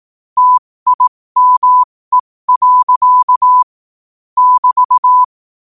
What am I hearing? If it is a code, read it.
Morse code, heard as a keyed tone, about 18 wpm, spaced as for 15 wpm, 1000 Hz. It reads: TIME. <BT>
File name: qotdfiles / 15wpm